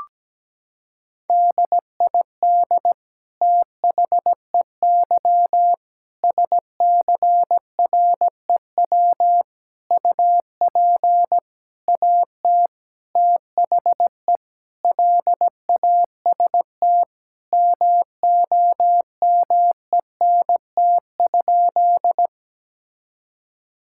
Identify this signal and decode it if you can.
Morse code, heard as a keyed tone, about 17 wpm, 700 Hz. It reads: DID THEY SCREW UP AT THE LAST MOMENT?
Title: Sentences from Top 1600 Words 17wpm